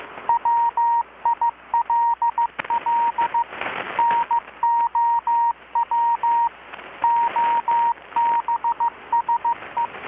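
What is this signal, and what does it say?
Morse code, heard as a keyed tone, about 15 wpm, 940 Hz, with radio noise. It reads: WILL NOW OBSE
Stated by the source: Morse from KPH's Night of Nights XX maritime radiotelegraphy event, received in Chiapas, Mexico about 3,800 km away on a Kenwood R-600 with a grounded 10 m long-wire antenna.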